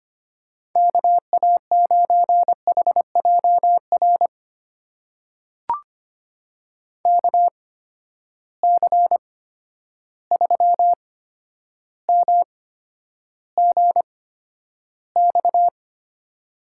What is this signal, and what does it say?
Morse code, heard as a keyed tone, about 25 wpm, spaced as for 11 wpm, 700 Hz. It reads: KA9HJR KC3MGX
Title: USA Call Signs - 6 Characters - 8x Character Spacing - 25wpm